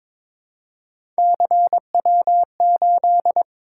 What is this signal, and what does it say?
Morse code, heard as a keyed tone, about 22 wpm, 700 Hz. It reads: CW8
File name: practice / 22wpm / Three Letter-Number Combinations - 22wpm